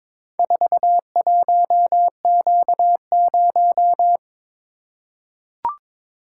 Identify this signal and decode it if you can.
Morse code, heard as a keyed tone, about 22 wpm, 700 Hz. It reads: 41Q0
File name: practice / 22wpm / Four Letter-Number Combinations - 22wpm